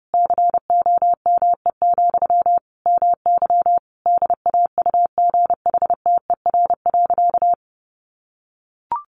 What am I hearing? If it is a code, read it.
Morse code, heard as a keyed tone, about 30 wpm, 700 Hz. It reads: COME, MY DAUGHTER.
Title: Sentences from Top 1500 Words 30wpm